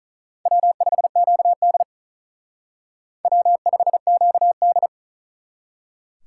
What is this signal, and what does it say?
Morse code, heard as a keyed tone, about 44 wpm, 700 Hz. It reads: W5QD W5QD